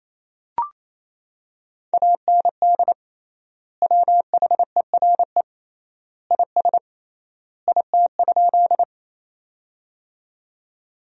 Morse code, heard as a keyed tone, about 28 wpm, 700 Hz. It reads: AND WHERE IS IT?
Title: 3x Word-Spacing - Sentences from Top 200 Words - 28wpm